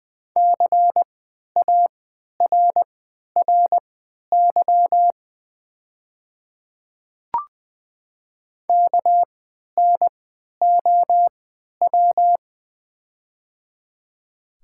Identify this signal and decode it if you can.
Morse code, heard as a keyed tone, about 20 wpm, spaced as for 11 wpm, 700 Hz. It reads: CARRY KNOW
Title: 3x Character Spacing - Top 500 Words - 20wpm